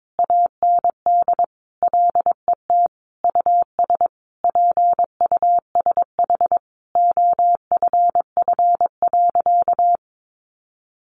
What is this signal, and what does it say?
Morse code, heard as a keyed tone, about 22 wpm, 700 Hz. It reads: AND LET US PUSH OFF.
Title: Sentences from Top 1300 Words 22wpm